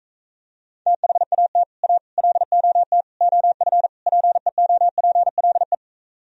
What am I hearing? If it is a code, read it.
Morse code, heard as a keyed tone, about 45 wpm, 700 Hz. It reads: THAT A LOT OF PEOPLE